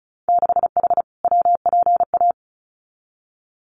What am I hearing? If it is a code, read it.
Morse code, heard as a keyed tone, about 34 wpm, 700 Hz. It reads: 6H WPA
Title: Winter Field Day Exchanges Encore 35wpm